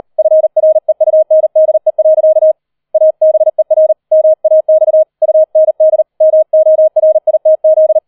Morse code, heard as a keyed tone, about 38 wpm, 630 Hz. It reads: FREUNDE. ABER MAX UND MORITZ
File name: 003 Dritter Streich 40wpm